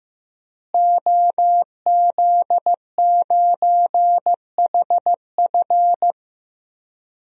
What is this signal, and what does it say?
Morse code, heard as a keyed tone, about 15 wpm, 700 Hz. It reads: OZ9HF
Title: DXCC-100 Call Signs - 5 Characters - Repeated 3x - 15wpm